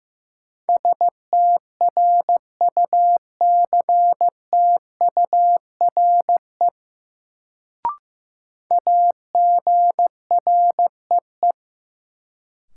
Morse code, heard as a keyed tone, about 15 wpm, 700 Hz. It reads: STRUCTURE AGREE
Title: Sentences from Top 1000 Words - Review Part 6 - New Word Drill - 15wpm